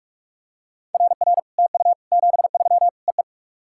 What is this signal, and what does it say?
Morse code, heard as a keyed tone, about 44 wpm, 700 Hz. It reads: RR TU 73 EE
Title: SOTA QSO Practice 45wpm